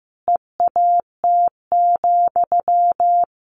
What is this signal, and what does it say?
Morse code, heard as a keyed tone, about 15 wpm, 700 Hz. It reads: EAT,
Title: Sentences from Top 1500 Words 15wpm